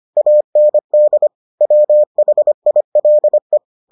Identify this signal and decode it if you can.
Morse code, heard as a keyed tone, about 25 wpm, 600 Hz. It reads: AND WHILE